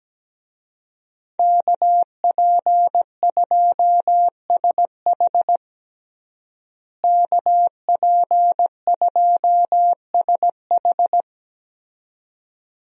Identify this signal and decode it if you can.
Morse code, heard as a keyed tone, about 17 wpm, 700 Hz. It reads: KP2SH KP2SH
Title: DXCC-100 Call Signs - 5 Characters - Repeated 2x - 17wpm